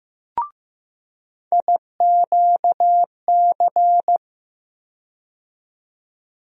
Morse code, heard as a keyed tone, about 15 wpm, 700 Hz. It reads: IQC
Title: USA Call Sign Postfixes - 3 Letters 15wpm